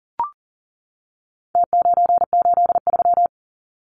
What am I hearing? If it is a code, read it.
Morse code, heard as a keyed tone, about 40 wpm, 700 Hz. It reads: T983
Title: Numbers - Contest Style - 4-Digit 40wpm